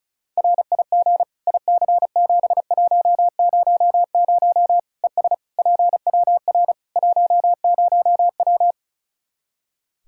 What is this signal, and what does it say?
Morse code, heard as a keyed tone, about 36 wpm, 700 Hz. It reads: RIG IC7100 ES PWR 10W
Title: K1USN SST QSO Practice 35wpm